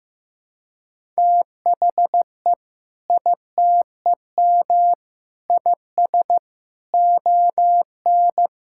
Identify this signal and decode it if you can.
Morse code, heard as a keyed tone, about 14 wpm, 700 Hz. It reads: THE ITEM IS ON
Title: Sentences from Top 1000 Words - Review Part 6 - 15wpm